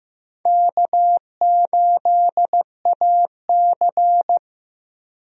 Morse code, heard as a keyed tone, about 15 wpm, 700 Hz. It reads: K8AC
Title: USA Call Signs - 4 Characters - Repeated 2x - 15wpm